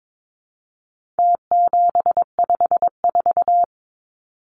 Morse code, heard as a keyed tone, about 22 wpm, 700 Hz. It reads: T754
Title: Numbers - Contest Style - 4-Digit 22wpm